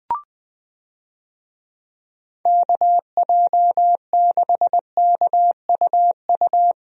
Morse code, heard as a keyed tone, about 20 wpm, 700 Hz. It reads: KJ6KUU
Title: USA Call Signs - 6 Characters - Repeated 3x - 20wpm